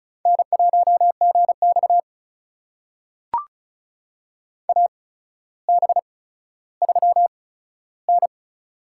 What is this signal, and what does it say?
Morse code, heard as a keyed tone, about 36 wpm, spaced as for 15 wpm, 700 Hz. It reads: N1GX AB3N
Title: USA Call Signs - 4 Characters - 8x Character Spacing - 35wpm